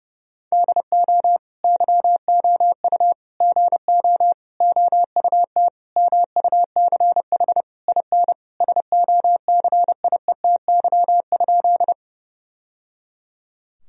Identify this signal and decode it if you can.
Morse code, heard as a keyed tone, about 32 wpm, 700 Hz. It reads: DO YOU GO OUT MUCH IN SOCIETY?
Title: Sentences from Top 1000 Words - Review Part 4 - 30wpm